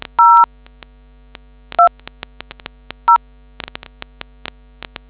DTMF, heard as telephone touch tones, tones of about 110 ms. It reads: 0 2 0